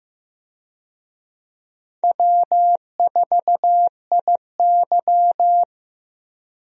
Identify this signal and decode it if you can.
Morse code, heard as a keyed tone, about 15 wpm, 700 Hz. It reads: W4IY
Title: USA Call Signs - 4 Characters - Repeated 3x - 15wpm